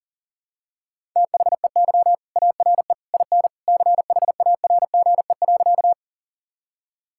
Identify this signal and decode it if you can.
Morse code, heard as a keyed tone, about 40 wpm, 700 Hz. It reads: THEY ARE IN CHARGE.
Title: Sentences from Top 1000 Words - Review Part 6 - 40wpm